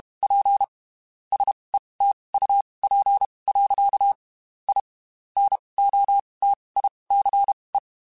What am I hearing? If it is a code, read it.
Morse code, heard as a keyed tone, about 32 wpm, spaced as for 23 wpm, 800 Hz. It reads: P SETUP. I NOTICE